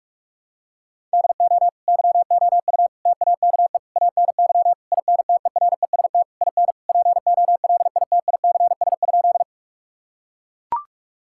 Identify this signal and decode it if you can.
Morse code, heard as a keyed tone, about 41 wpm, 700 Hz. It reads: DO YOU TAKE ANY INTEREST IN POLITICS?
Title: Sentences from Top 1000 Words - Review Part 6 - 45wpm